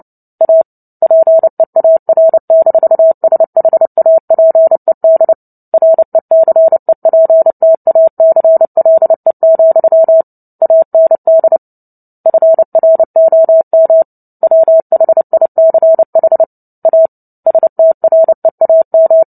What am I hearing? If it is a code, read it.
Morse code, heard as a keyed tone, about 30 wpm, 650 Hz. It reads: A PEAR-SHAPED RECEPTACLE, AND FROM WHICH A STREAM
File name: Book Two / 03-Chapter 3 - 29 WPM